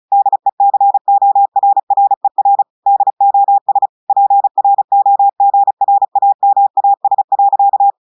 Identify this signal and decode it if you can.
Morse code, heard as a keyed tone, about 34 wpm, 800 Hz. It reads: DECORRER DOS PROGRAMAS.
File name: audios / portugais / portugais texte en clair 2 vitesse 35 mots minutes